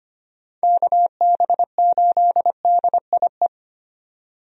Morse code, heard as a keyed tone, about 25 wpm, 700 Hz. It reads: KB8DIE
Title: USA Call Signs - 6 Characters - 25wpm